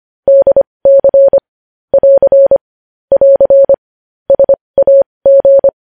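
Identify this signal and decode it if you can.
Morse code, heard as a keyed tone, about 25 wpm, spaced as for 20 wpm, 550 Hz. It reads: DC <AR> <AR> SAG
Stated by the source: Morse code news headlines practice at 20.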